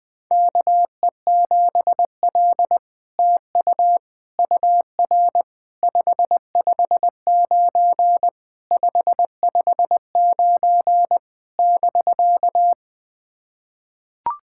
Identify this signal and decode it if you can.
Morse code, heard as a keyed tone, about 20 wpm, 700 Hz. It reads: KE7L TU UR 559 559 <BK>
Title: SOTA QSO Practice 20wpm